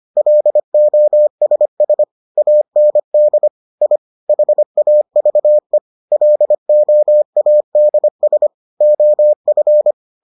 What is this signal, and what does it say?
Morse code, heard as a keyed tone, about 25 wpm, 600 Hz. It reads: LOSS AND I HAVE LOADS OF